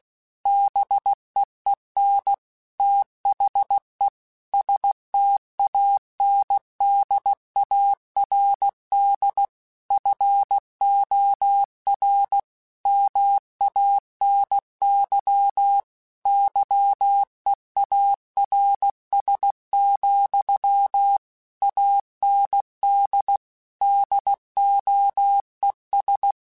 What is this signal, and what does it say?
Morse code, heard as a keyed tone, about 16 wpm, 800 Hz. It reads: BEEN THE STANDARD FOR MANY YEARS, AND DOES
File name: Technical Subjects / Technical Subjects-mp3s / AND Gate @15